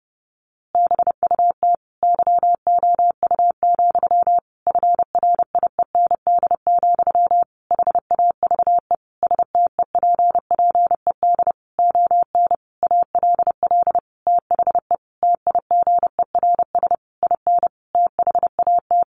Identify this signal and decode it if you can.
Morse code, heard as a keyed tone, about 30 wpm, 700 Hz. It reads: BUT YOU, FRIEND, HAVE STEPPED ON ALL THE TIGERS IN THAT
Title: A Sound Of Thunder 30wpm